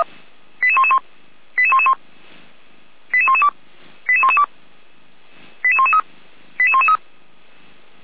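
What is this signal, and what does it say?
Five-tone selective call (ZVEI), heard as quick runs of five tones, tones of about 70 ms. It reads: 88111 88111 88112 88112 88113 88113
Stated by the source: Five-tone ZVEI selective call received on SDR, code 88485.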